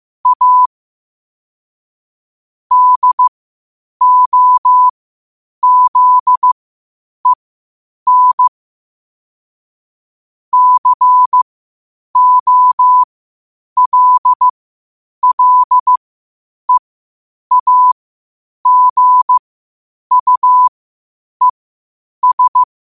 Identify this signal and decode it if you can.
Morse code, heard as a keyed tone, about 15 wpm, spaced as for 8 wpm, 1000 Hz. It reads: A DOZEN COLLEAGUES
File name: qotdfiles / 8wpm